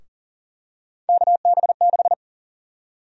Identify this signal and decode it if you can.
Morse code, heard as a keyed tone, about 40 wpm, 700 Hz. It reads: KB6
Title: USA Call Sign Prefixes - Repeated 2x - 40wpm